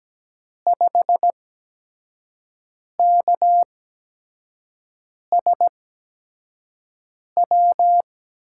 Morse code, heard as a keyed tone, about 17 wpm, spaced as for 5 wpm, 700 Hz. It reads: 5KSW
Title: USA Call Signs - 6 Characters - 8x Character Spacing - 17wpm